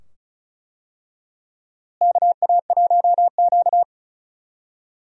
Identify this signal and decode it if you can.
Morse code, heard as a keyed tone, about 36 wpm, 700 Hz. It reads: KA1Q
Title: USA Call Signs - 4 Characters - Repeated 2x - 35wpm